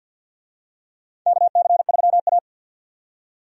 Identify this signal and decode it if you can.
Morse code, heard as a keyed tone, about 53 wpm, 700 Hz. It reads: KC3A
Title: USA Call Signs - 4 Characters - 50wpm